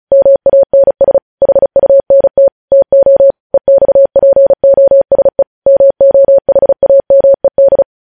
Morse code, heard as a keyed tone, about 36 wpm, 550 Hz. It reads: MANS HUNT TO EXPOSE MOHAMED